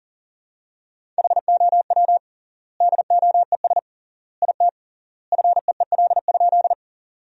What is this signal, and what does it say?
Morse code, heard as a keyed tone, about 37 wpm, 700 Hz. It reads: HOW DOES IT FEEL?
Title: 3x Word-Spacing - Sentences from Top 400 Words - 40wpm